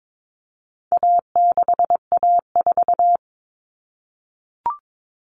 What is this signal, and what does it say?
Morse code, heard as a keyed tone, about 22 wpm, 700 Hz. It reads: A6A4
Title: Common Cut Numbers - 4-Digit 22wpm